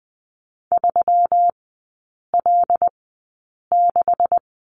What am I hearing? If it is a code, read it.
Morse code, heard as a keyed tone, about 20 wpm, spaced as for 8 wpm, 700 Hz. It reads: 3L6
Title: 133 Lesson 22e - Question Mark - Three Characters 20wpm